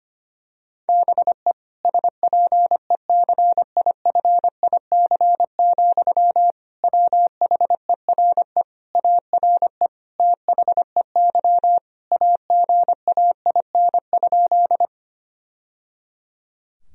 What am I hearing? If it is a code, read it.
Morse code, heard as a keyed tone, about 25 wpm, 700 Hz. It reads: BE SPECIFIC, WHERE ARE THEY AGAIN?
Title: Sentences from Top 1000 Words - Review Part 3 - 25wpm